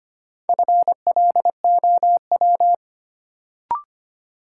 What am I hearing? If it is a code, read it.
Morse code, heard as a keyed tone, about 25 wpm, 700 Hz. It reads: FLOW
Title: Sentences from Top 1200 Words - New Word Drill - 25wpm